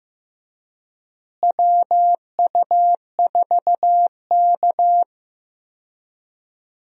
Morse code, heard as a keyed tone, about 15 wpm, 700 Hz. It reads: WU4K
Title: USA Call Signs - 4 Characters - Repeated 3x - 15wpm